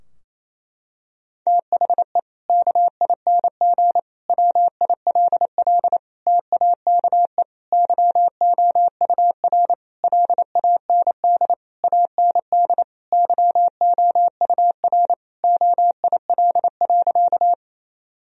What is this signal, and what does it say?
Morse code, heard as a keyed tone, about 28 wpm, 700 Hz. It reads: THE KING WILL TAKE YOUR LAND AND YOUR OIL.
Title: Sentences from Top 1000 Words - Review Part 4 - 28wpm